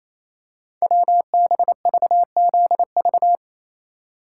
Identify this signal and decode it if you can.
Morse code, heard as a keyed tone, about 27 wpm, 700 Hz. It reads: WBVZV
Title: Code Groups - 5 Letters - 28wpm